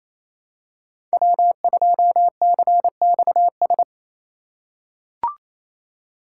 Morse code, heard as a keyed tone, about 27 wpm, 700 Hz. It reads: W2CXS
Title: Code Groups - 5 Letter:Numbers - 28wpm